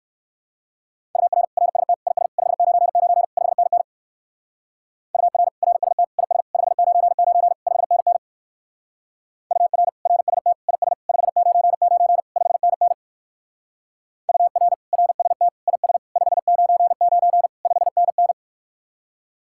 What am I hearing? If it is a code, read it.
Morse code, heard as a keyed tone, about 53 wpm, 700 Hz. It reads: UR RST IS 599 5NN UR RST IS 599 5NN UR RST IS 599 5NN UR RST IS 599 5NN
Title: Speed-Racing Common QSO Phrases - 45wpm